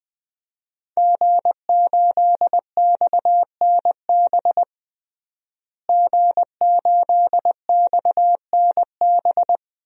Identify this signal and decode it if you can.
Morse code, heard as a keyed tone, about 20 wpm, 700 Hz. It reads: G8XNB G8XNB
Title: DXCC-100 Call Signs - 5 Characters - Repeated 2x - 20wpm